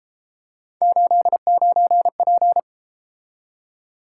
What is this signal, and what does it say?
Morse code, heard as a keyed tone, about 33 wpm, 700 Hz. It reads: 89P